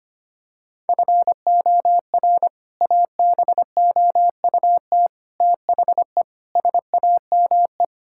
Morse code, heard as a keyed tone, about 25 wpm, 700 Hz. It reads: FOR ABOUT THE SAME
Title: Sentences from Top 1000 Words - Review Part 5 - 25wpm